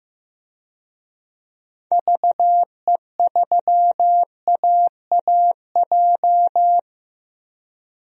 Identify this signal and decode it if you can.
Morse code, heard as a keyed tone, about 15 wpm, 700 Hz. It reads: VE3AAJ
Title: DXCC-100 Call Signs - 6 Characters - Repeated 3x - 15wpm